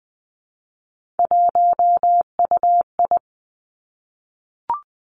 Morse code, heard as a keyed tone, about 20 wpm, 700 Hz. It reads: 1UI
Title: Three Challenging Letter-Number Combinations 20wpm